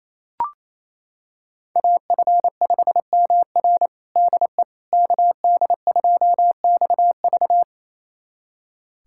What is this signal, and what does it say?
Morse code, heard as a keyed tone, about 28 wpm, 700 Hz. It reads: AF5MR DE KD2XV
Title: USA Callsign DE Callsign - 5 Characters - 28wpm